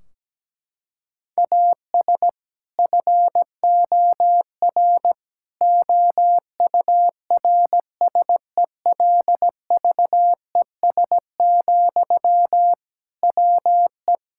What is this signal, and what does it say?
Morse code, heard as a keyed tone, about 17 wpm, 700 Hz. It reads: AS FOR OURSELVES, WE
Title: Sentences from Top 1600 Words 17wpm